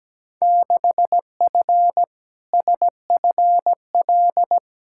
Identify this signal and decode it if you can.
Morse code, heard as a keyed tone, about 17 wpm, 700 Hz. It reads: 6F SFL
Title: Field Day Exchanges Encore - 17wpm